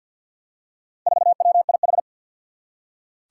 Morse code, heard as a keyed tone, about 47 wpm, 700 Hz. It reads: 4WIH
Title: Four Challenging Letter-Number Combinations 50wpm